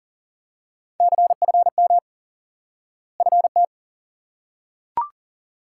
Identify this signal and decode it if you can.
Morse code, heard as a keyed tone, about 40 wpm, 700 Hz. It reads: CFM FT
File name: IQE Course 3 - Lesson 89 - YAGI - Two Elements - 40wpm